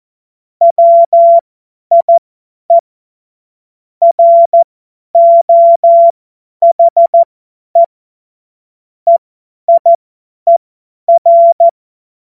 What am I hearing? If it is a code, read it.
Morse code, heard as a keyed tone, about 14 wpm, spaced as for 10 wpm, 680 Hz. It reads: WIE ROHE EIER
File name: Witz-15-Juli-2025-50-F70